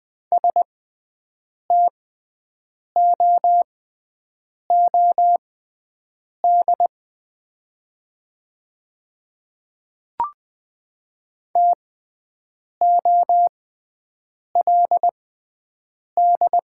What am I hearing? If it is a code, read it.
Morse code, heard as a keyed tone, about 20 wpm, spaced as for 7 wpm, 700 Hz. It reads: STOOD TOLD
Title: 6x Character Spacing - Top 500 Words - 20wpm